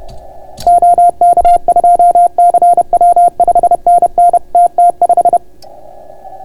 Morse code, heard as a keyed tone, about 31 wpm, 690 Hz. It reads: OK2CW 5NN TT5